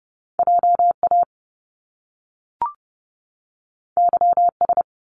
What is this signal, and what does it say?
Morse code, heard as a keyed tone, about 30 wpm, 700 Hz. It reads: JA YS
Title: Top Two Letter Combinations in Words 30wpm